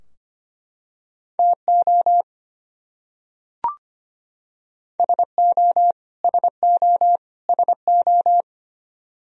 Morse code, heard as a keyed tone, about 25 wpm, 700 Hz. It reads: TO SO SO SO
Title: Top 100 Words Repeated 25wpm